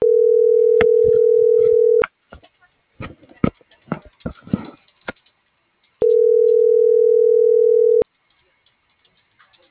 Ringback tone (North America), 440 + 480 Hz.